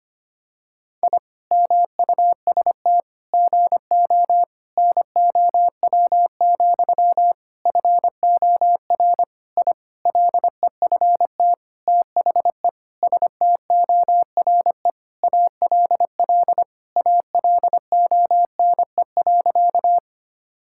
Morse code, heard as a keyed tone, about 25 wpm, 700 Hz. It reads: I MUST GO NOW, FOR I LEFT THE STORE ALL ALONE.
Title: Sentences from Top 1000 Words - Review Part 5 - 25wpm